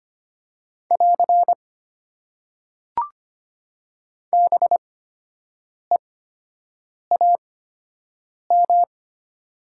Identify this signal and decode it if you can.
Morse code, heard as a keyed tone, about 25 wpm, spaced as for 7 wpm, 700 Hz. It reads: <AR> BEAM
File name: Top 90 QSO Elements - 8x Character Spacing - 25wpm